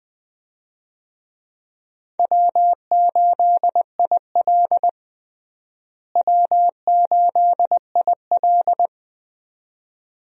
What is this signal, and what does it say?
Morse code, heard as a keyed tone, about 20 wpm, 700 Hz. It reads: W8IL W8IL
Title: USA Call Signs - 4 Characters - Repeated 2x - 20wpm